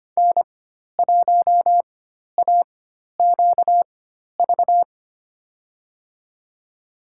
Morse code, heard as a keyed tone, about 25 wpm, spaced as for 12 wpm, 700 Hz. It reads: N1AQV
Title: USA Call Signs - 4x Character Spacing - 25wpm